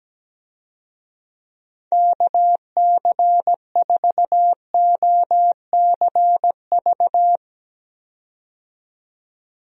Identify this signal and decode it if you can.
Morse code, heard as a keyed tone, about 17 wpm, 700 Hz. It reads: KC4OCV
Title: USA Call Signs - 6 Characters - 17wpm